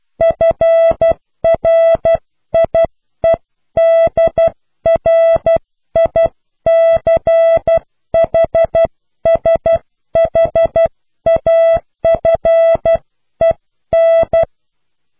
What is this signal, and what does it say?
Morse code, heard as a keyed tone, about 12 wpm, 660 Hz. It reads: FRIEDRICHSHAFEN
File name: cw / Pseudo woorden (15)